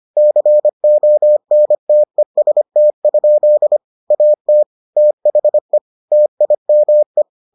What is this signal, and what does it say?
Morse code, heard as a keyed tone, about 25 wpm, 600 Hz. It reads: CONTEST? AT THE TIME